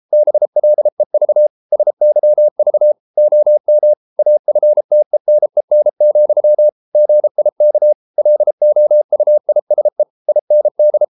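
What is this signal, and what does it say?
Morse code, heard as a keyed tone, about 33 wpm, 600 Hz. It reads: BLEV SYV OM AFTENEN, GIK LOUISE IND
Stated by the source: Morse code at 33 wpm, QRQ.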